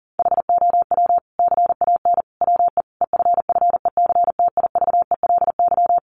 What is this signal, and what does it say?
Morse code, heard as a keyed tone, about 40 wpm, 700 Hz. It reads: HOW CAN WE EFFECTIVELY
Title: Sentences from Top 1400 Words 40wpm